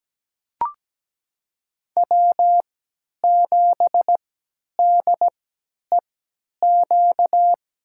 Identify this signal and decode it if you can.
Morse code, heard as a keyed tone, about 17 wpm, spaced as for 10 wpm, 700 Hz. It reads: W7DEQ
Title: USA Call Signs - 5 Characters - 3x Character Spacing - 17wpm